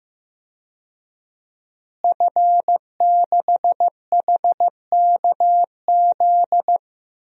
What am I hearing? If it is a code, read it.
Morse code, heard as a keyed tone, about 15 wpm, 700 Hz. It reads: F6HKZ
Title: DXCC-100 Call Signs - 5 Characters - Repeated 2x - 15wpm